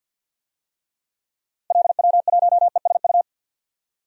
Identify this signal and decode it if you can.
Morse code, heard as a keyed tone, about 51 wpm, 700 Hz. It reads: LW1ESU
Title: DXCC-100 Call Signs - 6 Characters - Repeated 3x - 50wpm